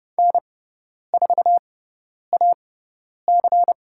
Morse code, heard as a keyed tone, about 30 wpm, spaced as for 10 wpm, 700 Hz. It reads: N4AC
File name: Lesson 18g - Letter m - Call Signs 30wpm